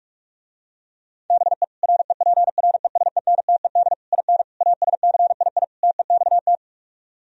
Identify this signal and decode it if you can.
Morse code, heard as a keyed tone, about 49 wpm, 700 Hz. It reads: BE REPRESENTED IN ASCII TEXT